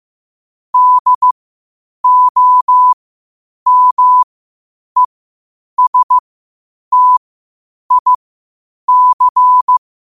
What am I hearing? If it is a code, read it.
Morse code, heard as a keyed tone, about 15 wpm, spaced as for 8 wpm, 1000 Hz. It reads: DOMESTIC
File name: qotdfiles / 8wpm